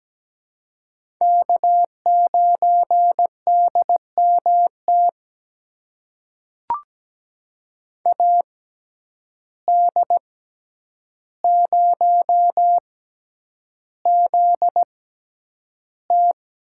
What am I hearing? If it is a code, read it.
Morse code, heard as a keyed tone, about 17 wpm, spaced as for 9 wpm, 700 Hz. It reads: K9DMT AD0ZT
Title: USA Call Signs - 5 Characters - 6x Character Spacing - 17wpm